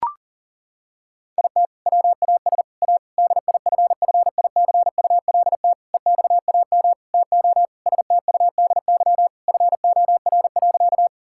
Morse code, heard as a keyed tone, about 37 wpm, 690 Hz. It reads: IT WAS A DIFFICULT EXAM TO STUDY FOR.
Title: Sentences from Top 1000 Words - Review Part 5 - 40wpm